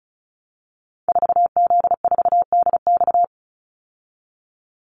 Morse code, heard as a keyed tone, about 34 wpm, 700 Hz. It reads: 4Z4DX
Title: Call Signs Repeated 3x - 35wpm